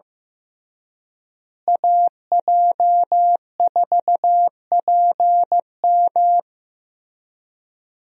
Morse code, heard as a keyed tone, about 15 wpm, 700 Hz. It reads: AJ4PM
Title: USA Call Signs - 5 Characters - Repeated 3x - 15wpm